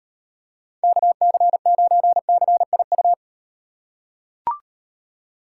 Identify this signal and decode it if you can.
Morse code, heard as a keyed tone, about 38 wpm, 700 Hz. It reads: KC9CIU